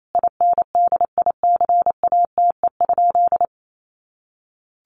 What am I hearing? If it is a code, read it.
Morse code, heard as a keyed tone, about 28 wpm, 700 Hz. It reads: INDICATE?
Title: Sentences from Top 1300 Words 28wpm